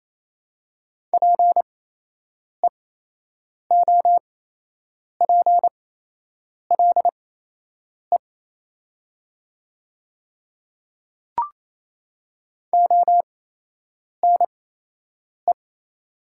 Morse code, heard as a keyed tone, about 28 wpm, spaced as for 8 wpm, 700 Hz. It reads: PEOPLE ONE
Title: 8x Character Spacing - Top 100 Words - 28wpm